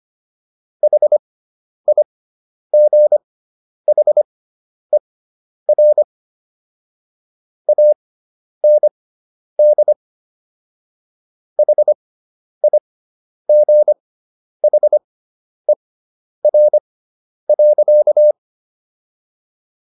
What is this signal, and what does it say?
Morse code, heard as a keyed tone, about 25 wpm, spaced as for 10 wpm, 600 Hz. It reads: HIGHER AND HIGHER.